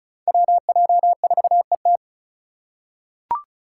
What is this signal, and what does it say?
Morse code, heard as a keyed tone, about 34 wpm, 700 Hz. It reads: WJ4ET
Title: USA Call Signs - 5 Characters - 2x Character Spacing - 35wpm